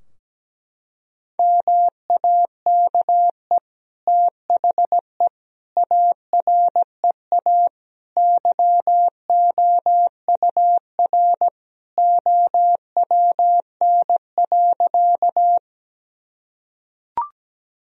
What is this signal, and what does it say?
Morse code, heard as a keyed tone, about 17 wpm, 700 Hz. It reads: MAKE THE AREA YOUR OWN.
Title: Sentences from Top 1000 Words - Review Part 2 - 17wpm